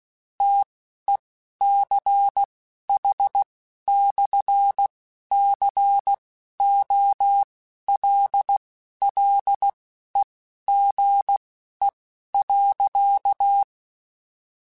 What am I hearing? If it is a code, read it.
Morse code, heard as a keyed tone, about 16 wpm, spaced as for 11 wpm, 800 Hz. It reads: TECH/COLLEGE.